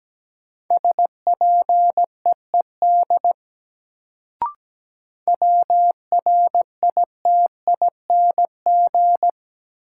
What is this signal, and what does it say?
Morse code, heard as a keyed tone, about 17 wpm, 700 Hz. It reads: SPEED WRITING
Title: Sentences from Top 800 Words - New Word Drill - 17wpm